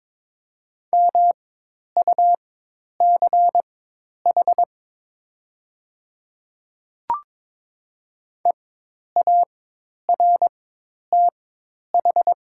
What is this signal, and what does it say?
Morse code, heard as a keyed tone, about 22 wpm, spaced as for 10 wpm, 700 Hz. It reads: MUCH EARTH
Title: 4x Character Spacing - Top 200 Words - 22wpm